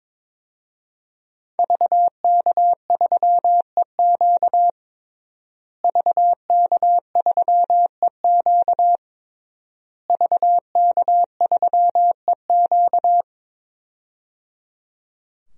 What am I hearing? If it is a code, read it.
Morse code, heard as a keyed tone, about 22 wpm, 700 Hz. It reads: VK3EQ VK3EQ VK3EQ
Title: DXCC-100 Call Signs - 5 Characters - Repeated 3x - 22wpm